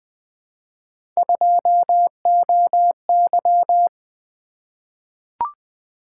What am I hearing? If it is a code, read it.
Morse code, heard as a keyed tone, about 20 wpm, 700 Hz. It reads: 2OY
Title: Three Letter-Number Combinations - 20wpm